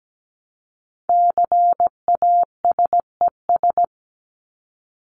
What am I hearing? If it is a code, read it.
Morse code, heard as a keyed tone, about 17 wpm, 700 Hz. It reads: CASES
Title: ES:S Suffix - Top 100 Words with Base Words 17wpm